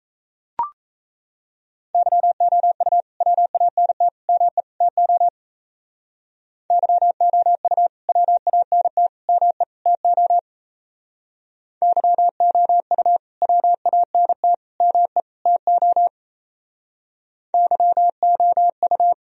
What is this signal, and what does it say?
Morse code, heard as a keyed tone, about 34 wpm, 700 Hz. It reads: YOU WANT ME TO YOU WANT ME TO YOU WANT ME TO YOU
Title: Speed-Racing - Sets of 4 Words 28wpm